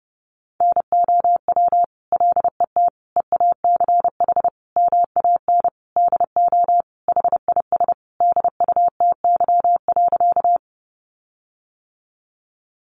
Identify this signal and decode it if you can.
Morse code, heard as a keyed tone, about 30 wpm, 700 Hz. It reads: NOW LET EACH MAN DO HIS DUTY.
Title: Sentences from Top 1500 Words 30wpm